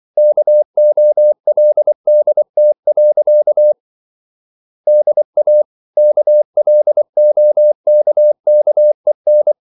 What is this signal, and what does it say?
Morse code, heard as a keyed tone, about 24 wpm, 600 Hz. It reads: KOLDT. DA KLOKKEN
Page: Ferie 24 wpm